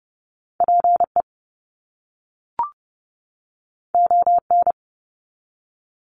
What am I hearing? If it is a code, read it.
Morse code, heard as a keyed tone, about 30 wpm, 700 Hz. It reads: PE ON
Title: Canadian Provinces and Territories 30wpm